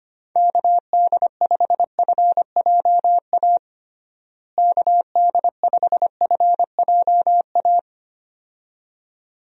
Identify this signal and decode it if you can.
Morse code, heard as a keyed tone, about 25 wpm, 700 Hz. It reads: KD5FJA KD5FJA
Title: USA Call Signs - 6 Characters - Repeated 2x - 25wpm